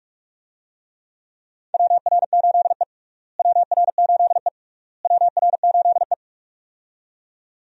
Morse code, heard as a keyed tone, about 43 wpm, 700 Hz. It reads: WR8E WR8E WR8E
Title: USA Call Signs - 4 Characters - Repeated 3x - 45wpm